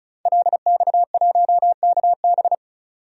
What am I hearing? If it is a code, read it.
Morse code, heard as a keyed tone, about 36 wpm, 700 Hz. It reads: LX1KB
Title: DXCC-100 Call Signs - 5 Characters - 35wpm